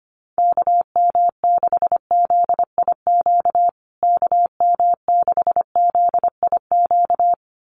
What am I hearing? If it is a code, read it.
Morse code, heard as a keyed tone, about 25 wpm, 700 Hz. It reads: KM6ZIQ KM6ZIQ
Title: USA Call Signs Repeated 2x 25wpm